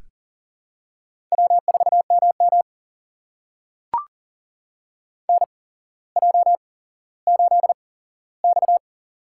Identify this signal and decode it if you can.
Morse code, heard as a keyed tone, about 40 wpm, spaced as for 17 wpm, 700 Hz. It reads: W4MM NJ8X
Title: USA Call Signs - 4 Characters - 8x Character Spacing - 40wpm